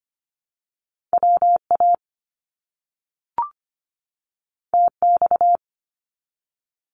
Morse code, heard as a keyed tone, about 25 wpm, 700 Hz. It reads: WA TX
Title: US States 25wpm